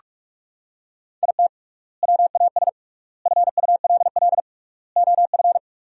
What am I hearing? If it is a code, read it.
Morse code, heard as a keyed tone, about 44 wpm, 700 Hz. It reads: IT WAS FULL OF
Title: 3x Word-Spacing - Sentences from Top 500 Words - 45wpm